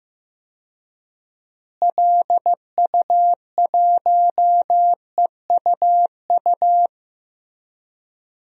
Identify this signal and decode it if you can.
Morse code, heard as a keyed tone, about 15 wpm, 700 Hz. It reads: LU1EUU
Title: DXCC-100 Call Signs - 6 Characters - Repeated 2x - 15wpm